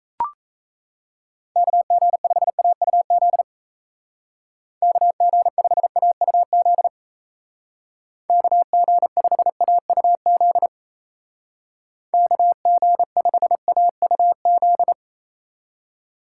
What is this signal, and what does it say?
Morse code, heard as a keyed tone, about 35 wpm, 700 Hz. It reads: KG5AUZ KG5AUZ KG5AUZ KG5AUZ
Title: Speed Racing - USA 2x3 Call Signs 28wpm